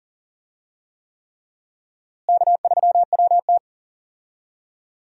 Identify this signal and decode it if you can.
Morse code, heard as a keyed tone, about 40 wpm, 700 Hz. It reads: K3WT
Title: USA Call Signs - 4 Characters - Repeated 3x - 40wpm